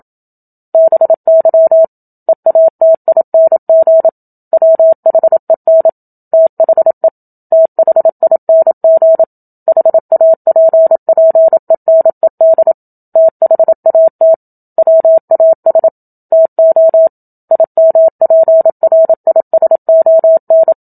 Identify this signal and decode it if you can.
Morse code, heard as a keyed tone, about 27 wpm, 650 Hz. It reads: BY EATING WHEN THE THING HAPPENED THAT WAS TO IMPRISON
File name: Book Two / 01-Chapter 1 - 27 WPM